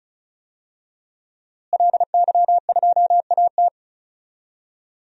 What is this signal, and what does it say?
Morse code, heard as a keyed tone, about 36 wpm, 700 Hz. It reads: LY2AT
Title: DXCC-100 Call Signs - 5 Characters - Repeated 2x - 35wpm